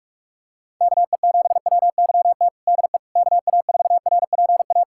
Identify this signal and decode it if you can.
Morse code, heard as a keyed tone, about 42 wpm, 700 Hz. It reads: KE7WYT DE KA4RPA
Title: USA Callsign DE Callsign 45wpm